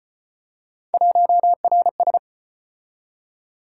Morse code, heard as a keyed tone, about 35 wpm, 700 Hz. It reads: 1RS